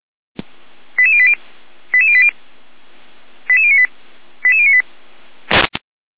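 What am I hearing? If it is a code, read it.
Five-tone selective call (ZVEI), heard as quick runs of five tones, tones of about 70 ms. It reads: 88080 88080 88098 88098
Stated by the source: ZVEI five-tone selective call received on an SDR.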